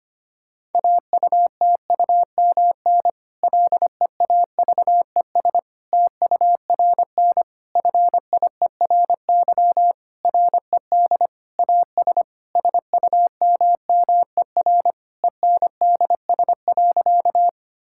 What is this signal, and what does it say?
Morse code, heard as a keyed tone, about 25 wpm, 700 Hz. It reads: AUTUMN LEAVES TURN FIERY RED AS SUMMER ENDS.
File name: practice / 25wpm / Difficult Sentences with Hints - 25wpm